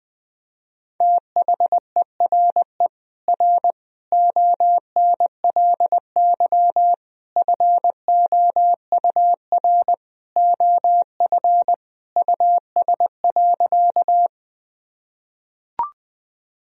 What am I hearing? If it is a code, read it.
Morse code, heard as a keyed tone, about 21 wpm, 700 Hz. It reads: THERE R ONLY FOUR OF US.
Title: Sentences from Top 300 Words and Abbreviations 20wpm